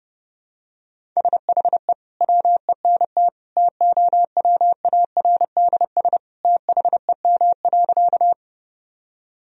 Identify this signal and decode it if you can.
Morse code, heard as a keyed tone, about 32 wpm, 700 Hz. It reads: SHE WENT TOWARDS THEM.
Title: Sentences from Top 1000 Words - Review Part 2 - 30wpm